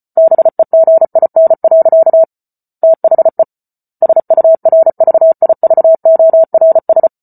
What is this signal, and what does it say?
Morse code, heard as a keyed tone, about 34 wpm, 650 Hz. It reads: BEGIN. THE SURVIVORS